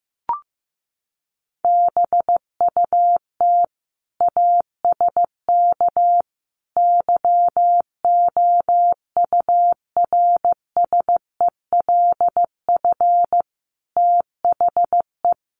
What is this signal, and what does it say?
Morse code, heard as a keyed tone, about 15 wpm, 700 Hz. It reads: BUT ASK YOURSELF THE
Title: Sentences from Top 600 Words 15wpm